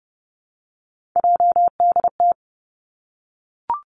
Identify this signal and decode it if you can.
Morse code, heard as a keyed tone, about 30 wpm, 700 Hz. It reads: JDT
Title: Three Challenging Letter-Number Combinations 30wpm